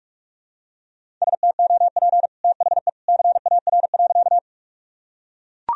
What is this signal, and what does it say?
Morse code, heard as a keyed tone, about 43 wpm, 700 Hz. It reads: STOP THE CAR.
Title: Sentences from Top 1000 Words - Review Part 1 - 45wpm